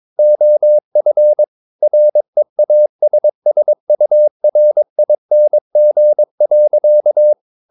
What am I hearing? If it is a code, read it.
Morse code, heard as a keyed tone, about 22 wpm, 600 Hz. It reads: OF REASSURING.
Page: CW med Gnister